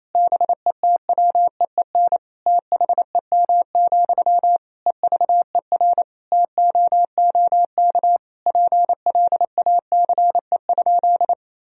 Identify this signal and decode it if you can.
Morse code, heard as a keyed tone, about 28 wpm, 700 Hz. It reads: BETWEEN THEM, EVER TOOK PLACE?
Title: Sentences from Top 1000 Words - Review Part 1 - 28wpm